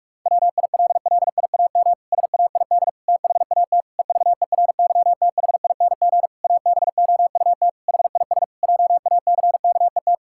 Morse code, heard as a keyed tone, about 46 wpm, 700 Hz. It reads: WILLIAM SAID THAT EVERYTHING ABOUT HIS JACKET
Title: Pangram Challenge v2 - 45wpm